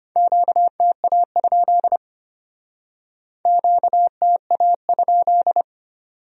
Morse code, heard as a keyed tone, about 33 wpm, 700 Hz. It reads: QTA? QTA?